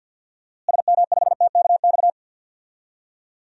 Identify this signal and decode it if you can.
Morse code, heard as a keyed tone, about 50 wpm, 700 Hz. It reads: SM5TKX
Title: DXCC-100 Call Signs - 6 Characters - Repeated 2x - 50wpm